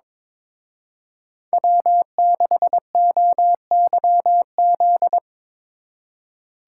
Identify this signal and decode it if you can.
Morse code, heard as a keyed tone, about 23 wpm, 700 Hz. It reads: W6OYZ
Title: Speed Racing - USA 1x3 Call Signs 22wpm